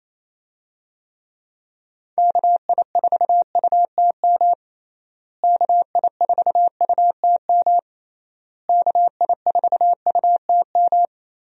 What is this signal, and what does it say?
Morse code, heard as a keyed tone, about 27 wpm, 700 Hz. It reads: KI4UTM KI4UTM KI4UTM
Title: USA Call Signs - 6 Characters - Repeated 3x - 28wpm